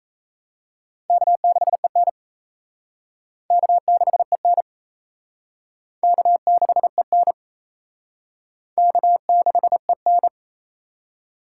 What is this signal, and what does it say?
Morse code, heard as a keyed tone, about 34 wpm, 700 Hz. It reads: K6EN K6EN K6EN K6EN
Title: Speed Racing - USA 1x2 Call Signs 28wpm